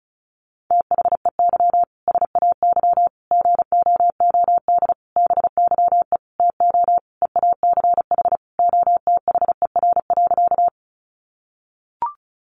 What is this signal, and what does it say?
Morse code, heard as a keyed tone, about 34 wpm, 700 Hz. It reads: THEY SAY GOOD BYE TO EACH OTHER.
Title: Easy Phrases 35wpm